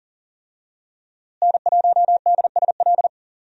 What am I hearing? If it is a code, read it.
Morse code, heard as a keyed tone, about 40 wpm, 700 Hz. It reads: N1DSL
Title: USA Call Signs - 5 Characters - Repeated 2x - 40wpm